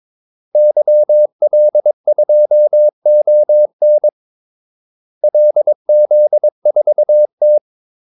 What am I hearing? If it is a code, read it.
Morse code, heard as a keyed tone, about 22 wpm, 600 Hz. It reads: YL2ON LZ4T
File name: Callsigns-22wpm